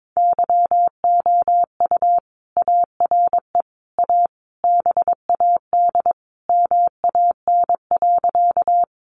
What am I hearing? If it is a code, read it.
Morse code, heard as a keyed tone, about 22 wpm, 700 Hz. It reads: YOU ARE A BAD MAN.
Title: Sentences from Top 700 Words 22wpm